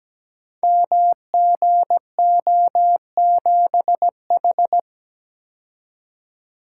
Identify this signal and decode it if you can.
Morse code, heard as a keyed tone, about 17 wpm, 700 Hz. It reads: MGO7H
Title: Code Groups - 5 Letter:Numbers - 17wpm